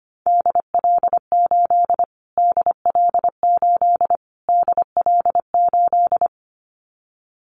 Morse code, heard as a keyed tone, about 25 wpm, 700 Hz. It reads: DL8 DL8 DL8
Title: Callsign Prefixes for Top 25 DXCC - Repeated 3x - 25wpm